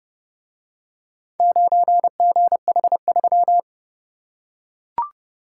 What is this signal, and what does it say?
Morse code, heard as a keyed tone, about 29 wpm, 700 Hz. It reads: 9GH3